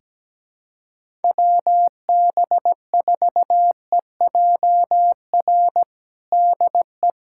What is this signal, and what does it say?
Morse code, heard as a keyed tone, about 17 wpm, 700 Hz. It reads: WB4EJR DE
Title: USA Callsign DE Callsign 17wpm